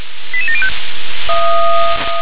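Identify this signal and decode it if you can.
Five-tone selective call (ZVEI), heard as quick runs of five tones, tones of about 70 ms. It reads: 88705
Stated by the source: ZVEI five-tone selective call 88840 received by SDR.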